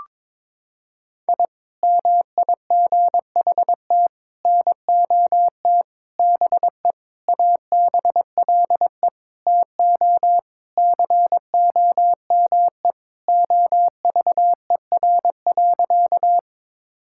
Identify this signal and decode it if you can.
Morse code, heard as a keyed tone, about 21 wpm, 700 Hz. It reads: I MIGHT NOT BE ABLE TO COME OVER.
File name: practice / 22wpm / Sentences with 4-Word Phrases - 22wpm